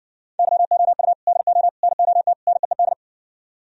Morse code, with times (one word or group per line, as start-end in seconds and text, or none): 0.4-1.1 YOU
1.3-1.7 DO
1.8-2.3 NOT
2.5-2.9 NEED